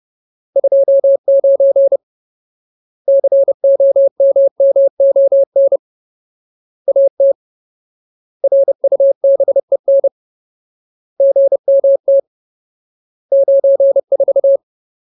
Morse code, heard as a keyed tone, about 30 wpm, 550 Hz. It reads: 29 COMMON AT RUBEN GMT 94
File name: Ultimate Word List / 30WPM Ultimate